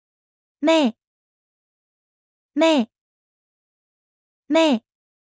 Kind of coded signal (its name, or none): none